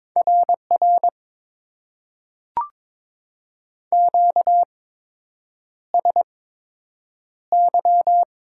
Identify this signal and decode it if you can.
Morse code, heard as a keyed tone, about 22 wpm, spaced as for 8 wpm, 700 Hz. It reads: RR QSY
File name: Top 90 QSO Elements - 8x Character Spacing - 22wpm